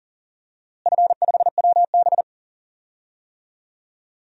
Morse code, heard as a keyed tone, about 37 wpm, 700 Hz. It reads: F5WB
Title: DXCC-100 Call Signs - 4 Characters - Repeated 2x - 40wpm